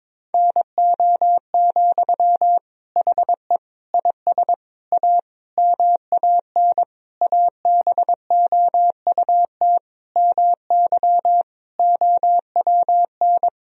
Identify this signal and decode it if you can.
Morse code, heard as a keyed tone, about 22 wpm, 700 Hz. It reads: NO, HE IS A MAN ABOUT MY OWN
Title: Sentences from Top 1000 Words - Review Part 2 - 22wpm